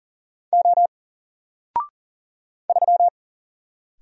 Morse code, heard as a keyed tone, about 40 wpm, 700 Hz. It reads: O 3
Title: Single Character (Letter:Number:Punctuation:Prosign) 40wpm